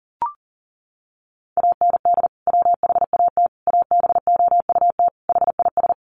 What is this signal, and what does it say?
Morse code, heard as a keyed tone, about 40 wpm, 700 Hz. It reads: AND WHAT ABOUT HIS
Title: Sentences from Top 1200 Words 40wpm